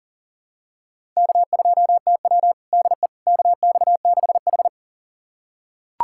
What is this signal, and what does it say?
Morse code, heard as a keyed tone, about 40 wpm, 700 Hz. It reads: K2TW DE KX6H
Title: USA Callsign DE Callsign - 4 Characters - 40wpm